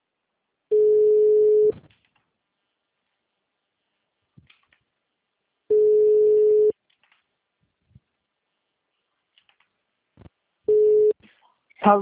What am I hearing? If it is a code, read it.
Ringback tone (Europe, 425 Hz), with speech.